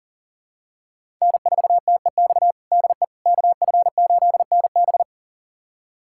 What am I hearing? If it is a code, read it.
Morse code, heard as a keyed tone, about 37 wpm, 700 Hz. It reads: N4TEX DE KF8NB
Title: USA Callsign DE Callsign - 5 Characters - 40wpm